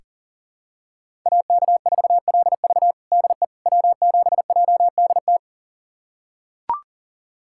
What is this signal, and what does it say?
Morse code, heard as a keyed tone, about 40 wpm, 700 Hz. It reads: AK4LV DE W7JDT
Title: USA Callsign DE Callsign - 5 Characters - 40wpm